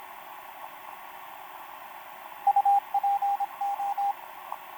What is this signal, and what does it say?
Morse code, heard as a keyed tone, about 26 wpm, 800 Hz, with radio noise. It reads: UPO
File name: pj4-w9njy f5upo etc 12m cw 28mar2011 1924